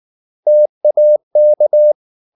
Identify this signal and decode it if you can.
Morse code, heard as a keyed tone, about 19 wpm, 600 Hz. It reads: TAK
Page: Ferie 19 wpm